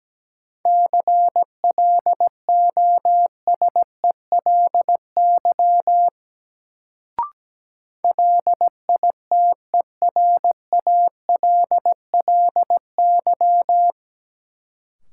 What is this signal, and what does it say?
Morse code, heard as a keyed tone, about 17 wpm, 700 Hz. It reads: CLOSELY LITERALLY
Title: Sentences from Top 1400 Words - New Word Drill Alt - 17wpm